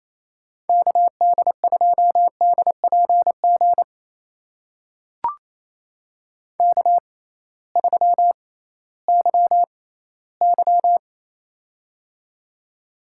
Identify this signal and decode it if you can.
Morse code, heard as a keyed tone, about 28 wpm, spaced as for 16 wpm, 700 Hz. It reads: KD2DPG K3YY
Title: USA Call Signs - 6x Character Spacing - 28wpm